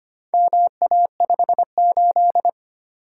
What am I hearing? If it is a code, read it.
Morse code, heard as a keyed tone, about 25 wpm, 700 Hz. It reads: MA58
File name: practice / 25wpm / Four Letter-Number Combinations - 25wpm